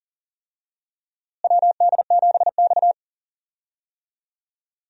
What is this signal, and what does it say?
Morse code, heard as a keyed tone, about 37 wpm, 700 Hz. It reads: WD7X
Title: USA Call Signs - 4 Characters - Repeated 2x - 40wpm